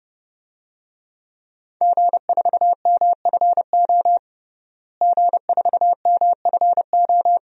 Morse code, heard as a keyed tone, about 29 wpm, 700 Hz. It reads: G4MFO G4MFO
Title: DXCC-100 Call Signs - 5 Characters - Repeated 2x - 30wpm